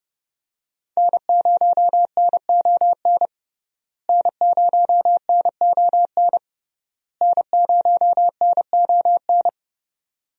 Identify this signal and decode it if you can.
Morse code, heard as a keyed tone, about 32 wpm, 700 Hz. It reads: N0NON N0NON N0NON
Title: USA Call Signs - 5 Characters - Repeated 3x - 30wpm